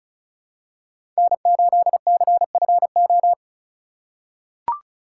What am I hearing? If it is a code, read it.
Morse code, heard as a keyed tone, about 36 wpm, 700 Hz. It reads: N8CFO
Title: USA Call Signs - 5 Characters - 4x Character Spacing - 35wpm